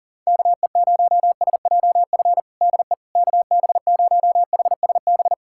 Morse code, heard as a keyed tone, about 37 wpm, 700 Hz. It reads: KE0SJF DE KB0HSB
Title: USA Callsign DE Callsign - 6 Characters - 40wpm